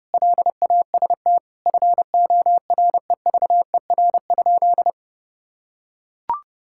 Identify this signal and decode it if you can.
Morse code, heard as a keyed tone, about 30 wpm, 700 Hz. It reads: LAST FOREVER?
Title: Sentences from Top 1600 Words 30wpm